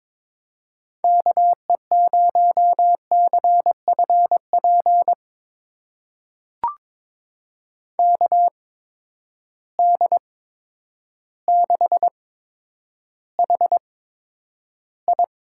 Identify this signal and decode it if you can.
Morse code, heard as a keyed tone, about 23 wpm, spaced as for 10 wpm, 700 Hz. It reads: KE0CFP KD6HI
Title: USA Call Signs - 8x Character Spacing - 22wpm